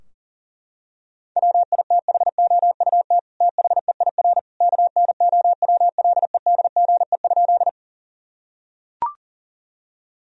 Morse code, heard as a keyed tone, about 40 wpm, 700 Hz. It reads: WITHOUT THEIR KNOWLEDGE?
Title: Sentences from Top 1000 Words - Review Part 3 - 40wpm